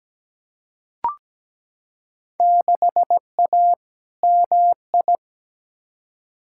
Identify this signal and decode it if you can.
Morse code, heard as a keyed tone, about 17 wpm, 700 Hz. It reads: 6A MI
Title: Field Day Exchanges - 17wpm